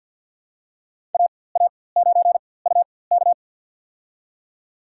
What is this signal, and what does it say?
Morse code, heard as a keyed tone, about 52 wpm, spaced as for 24 wpm, 700 Hz. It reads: AA9UK
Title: USA Call Signs - 5 Characters - 4x Character Spacing - 50wpm